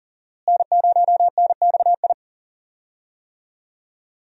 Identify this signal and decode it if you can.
Morse code, heard as a keyed tone, about 40 wpm, 700 Hz. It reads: N0NXI
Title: USA Call Signs - 5 Characters - Repeated 3x - 40wpm